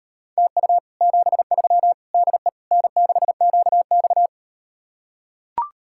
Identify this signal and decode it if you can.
Morse code, heard as a keyed tone, about 39 wpm, 700 Hz. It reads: TU 73 DE N6QX